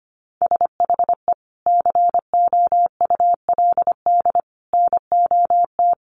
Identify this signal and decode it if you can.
Morse code, heard as a keyed tone, about 25 wpm, 700 Hz. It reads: SHE COULD NOT
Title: Sentences from Top 700 Words 25wpm